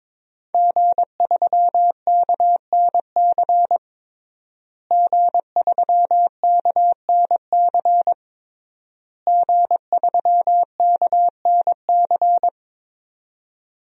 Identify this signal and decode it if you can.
Morse code, heard as a keyed tone, about 22 wpm, 700 Hz. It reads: G3KNC G3KNC G3KNC
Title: DXCC-100 Call Signs - 5 Characters - Repeated 3x - 22wpm